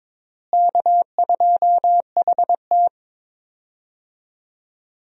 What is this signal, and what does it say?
Morse code, heard as a keyed tone, about 22 wpm, 700 Hz. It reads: K2HT
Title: USA Call Signs - 4 Characters - Repeated 3x - 22wpm